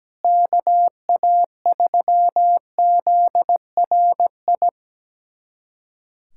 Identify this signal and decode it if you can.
Morse code, heard as a keyed tone, about 17 wpm, 700 Hz. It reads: KA3ZRI
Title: USA Callsign DE Callsign - 6 Characters - 17wpm